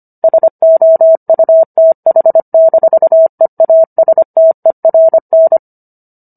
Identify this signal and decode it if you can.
Morse code, heard as a keyed tone, about 25 wpm, 650 Hz. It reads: SOUTH-EASTERN